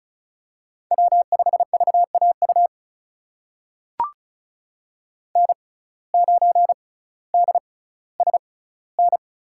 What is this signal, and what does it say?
Morse code, heard as a keyed tone, about 36 wpm, spaced as for 18 wpm, 700 Hz. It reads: W5VAU N9DSN
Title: USA Call Signs - 5 Characters - 6x Character Spacing - 35wpm